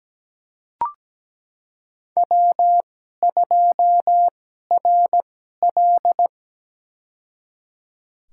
Morse code, heard as a keyed tone, about 17 wpm, spaced as for 12 wpm, 700 Hz. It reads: W2RL
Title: USA Call Signs - 4 Characters - 2x Character Spacing - 17wpm